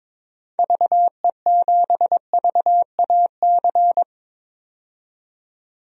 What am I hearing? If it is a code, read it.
Morse code, heard as a keyed tone, about 22 wpm, 700 Hz. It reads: VE7VAC
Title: DXCC-100 Call Signs - 6 Characters - 22wpm